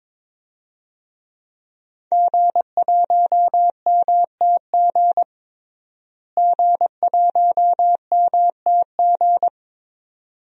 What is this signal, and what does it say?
Morse code, heard as a keyed tone, about 22 wpm, 700 Hz. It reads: G1MTG G1MTG
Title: DXCC-100 Call Signs - 5 Characters - Repeated 2x - 22wpm